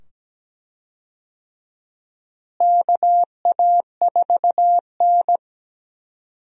Morse code, heard as a keyed tone, about 17 wpm, 700 Hz. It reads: KA4N
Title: USA Call Signs - 4 Characters - Repeated 2x - 17wpm